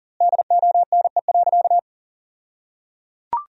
Morse code, heard as a keyed tone, about 37 wpm, 700 Hz. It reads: DONE.
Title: 2x Word-Spacing - Sentences from Top 500 Words - 40wpm